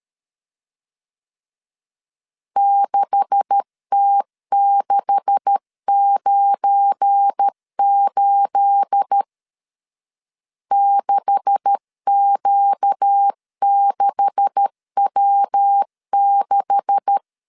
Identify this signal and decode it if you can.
Morse code, heard as a keyed tone, about 13 wpm, 800 Hz. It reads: 6T698 6Q6W6
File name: LEZIONI / 9 286 12wpm 2305